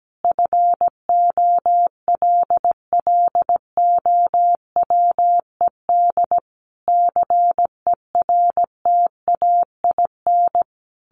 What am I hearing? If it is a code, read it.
Morse code, heard as a keyed tone, about 17 wpm, 700 Hz. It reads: FOLLOWED CERTAIN
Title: A Sound Of Thunder 17wpm